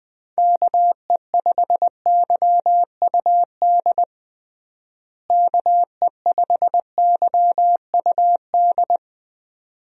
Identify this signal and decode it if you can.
Morse code, heard as a keyed tone, about 20 wpm, 700 Hz. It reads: KE5YUD KE5YUD
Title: USA Call Signs - 6 Characters - Repeated 2x - 20wpm